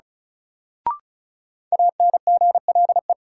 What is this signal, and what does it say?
Morse code, heard as a keyed tone, about 34 wpm, 700 Hz. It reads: ANGLE
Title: Sentences from Top 1500 Words - New Word Drill - 35wpm